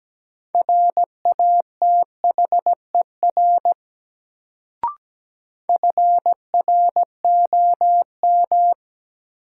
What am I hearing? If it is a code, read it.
Morse code, heard as a keyed tone, about 17 wpm, 700 Hz. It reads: RATHER FROM
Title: Sets of 2 Words - New Word Drill - 17wpm